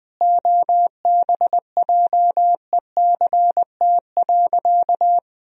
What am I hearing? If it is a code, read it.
Morse code, heard as a keyed tone, about 20 wpm, 700 Hz. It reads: OBJECT.
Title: Sentences from Top 1000 Words - Review Part 3 - 20wpm